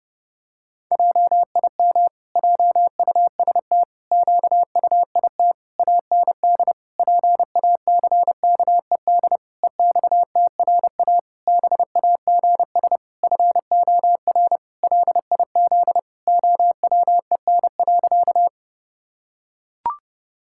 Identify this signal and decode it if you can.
Morse code, heard as a keyed tone, about 30 wpm, 700 Hz. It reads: JIM JUST QUIT AND PACKED EXTRA BAGS FOR LIZ OWEN.
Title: Pangram Challenge v2 - 30wpm